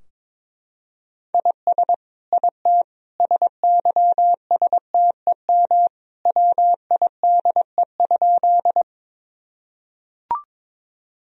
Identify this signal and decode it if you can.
Morse code, heard as a keyed tone, about 22 wpm, 700 Hz. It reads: IS IT SYSTEM WIDE?
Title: Sentences from Top 1000 Words - Review Part 3 - 22wpm